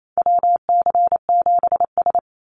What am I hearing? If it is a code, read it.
Morse code, heard as a keyed tone, about 28 wpm, 700 Hz. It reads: WC7S
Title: Call Signs - 28wpm